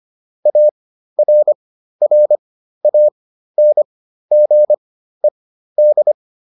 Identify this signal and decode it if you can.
Morse code, heard as a keyed tone, about 25 wpm, spaced as for 13 wpm, 600 Hz. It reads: ARRANGED